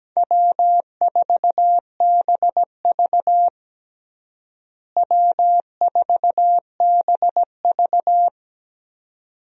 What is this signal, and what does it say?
Morse code, heard as a keyed tone, about 17 wpm, 700 Hz. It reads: W4BV W4BV
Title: USA Call Signs - 4 Characters - Repeated 2x - 17wpm